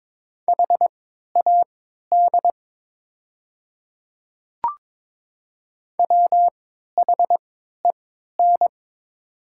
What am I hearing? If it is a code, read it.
Morse code, heard as a keyed tone, about 22 wpm, spaced as for 12 wpm, 690 Hz. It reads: HAD WHEN
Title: 3x Character Spacing - Top 100 Words - 22wpm